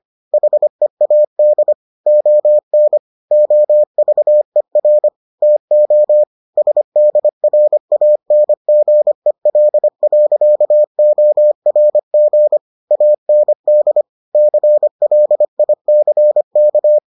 Morse code, heard as a keyed tone, about 25 wpm, 600 Hz. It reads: HEAD ON OVER TO SDRANGEL.ORG AND CLICK